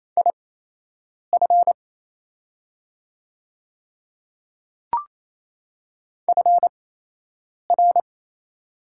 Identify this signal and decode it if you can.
Morse code, heard as a keyed tone, about 28 wpm, spaced as for 8 wpm, 700 Hz. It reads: IF FR
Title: 8x Character Spacing - Top 100 Words - 28wpm